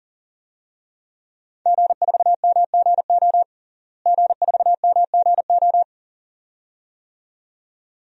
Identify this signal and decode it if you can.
Morse code, heard as a keyed tone, about 40 wpm, 700 Hz. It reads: G4MGO G4MGO
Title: DXCC-100 Call Signs - 5 Characters - Repeated 2x - 40wpm